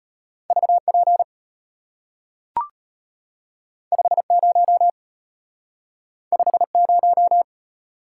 Morse code, heard as a keyed tone, about 34 wpm, 700 Hz. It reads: VP 50 50